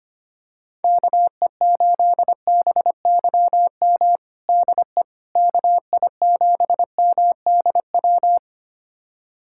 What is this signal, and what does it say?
Morse code, heard as a keyed tone, about 25 wpm, 700 Hz. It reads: KE8BYM DE KI7MDW
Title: USA Callsign DE Callsign - 6 Characters - 25wpm